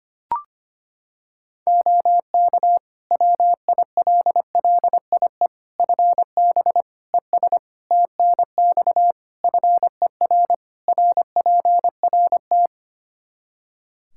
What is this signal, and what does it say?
Morse code, heard as a keyed tone, about 25 wpm, 700 Hz. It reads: OK WILLIE FB ES TNX FER RPRT
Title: Practice QSO - Exchange 2 - Info - 25wpm